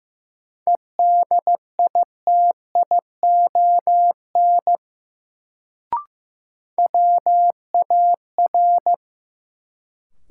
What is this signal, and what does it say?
Morse code, heard as a keyed tone, about 15 wpm, 700 Hz. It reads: EDITION WAR
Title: Sets of 5 Words - New Word Drill - 15wpm